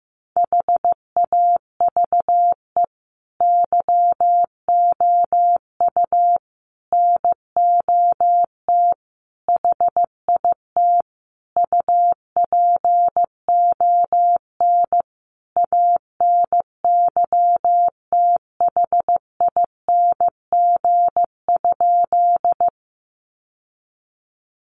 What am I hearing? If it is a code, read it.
Morse code, heard as a keyed tone, about 15 wpm, 700 Hz. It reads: HAVE YOU NOT HIT UPON ANYTHING?
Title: Sentences from Top 1200 Words 15wpm